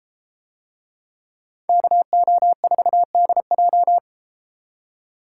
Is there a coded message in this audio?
Yes: Morse code from 1.7 to 4.0 s, heard as a keyed tone, about 33 wpm, 700 Hz. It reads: KO4DJ